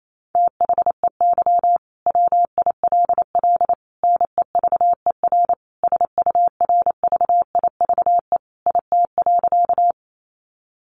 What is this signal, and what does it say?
Morse code, heard as a keyed tone, about 28 wpm, 700 Hz. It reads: THEY WILL NEVER SURVIVE IT.
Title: Sentences from Top 1300 Words 28wpm